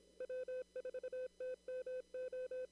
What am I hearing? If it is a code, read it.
Morse code, heard as a keyed tone, about 27 wpm, 510 Hz. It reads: W4TMO
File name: CW / kt5j wpx cw-op-k5pi